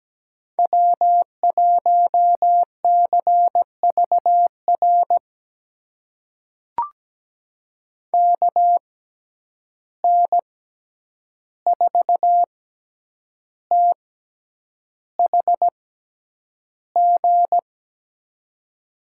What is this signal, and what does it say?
Morse code, heard as a keyed tone, about 17 wpm, spaced as for 8 wpm, 700 Hz. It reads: W1CVR KN4THG
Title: USA Call Signs - 6x Character Spacing - 17wpm